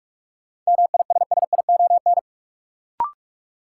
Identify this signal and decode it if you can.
Morse code, heard as a keyed tone, about 46 wpm, 700 Hz. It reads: MISSION
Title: Sentences from Top 1600 Words - New Word Drill Alt - 45wpm